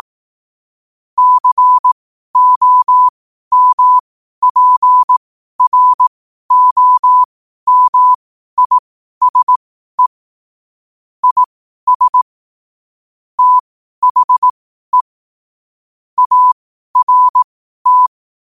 Morse code, heard as a keyed tone, about 18 wpm, spaced as for 12 wpm, 1000 Hz. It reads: COMPROMISE IS THE ART
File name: 12wpm